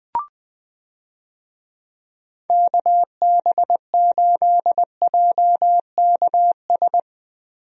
Morse code, heard as a keyed tone, about 20 wpm, 700 Hz. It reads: KB8JKS
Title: USA Call Signs - 6 Characters - Repeated 2x - 20wpm